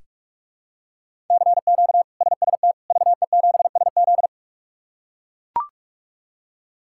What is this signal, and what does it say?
Morse code, heard as a keyed tone, about 44 wpm, 700 Hz. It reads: CQ SST VE7SZ
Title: K1USN SST QSO Practice - 45wpm